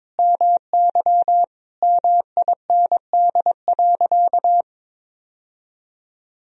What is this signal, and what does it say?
Morse code, heard as a keyed tone, about 22 wpm, 700 Hz. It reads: MY MIND.
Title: Sentences from Top 1000 Words - Review Part 2 - 22wpm